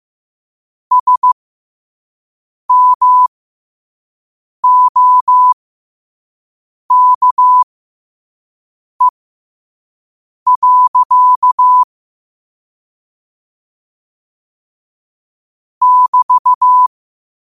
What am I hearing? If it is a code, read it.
Morse code, heard as a keyed tone, about 15 wpm, spaced as for 5 wpm, 1000 Hz. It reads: SMOKE. <BT>
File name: qotdfiles / 5wpm